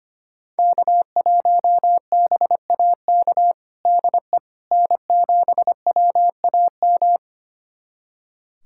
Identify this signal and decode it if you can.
Morse code, heard as a keyed tone, about 25 wpm, 700 Hz. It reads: K1BAK DE N7WAM
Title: USA Callsign DE Callsign - 5 Characters - 25wpm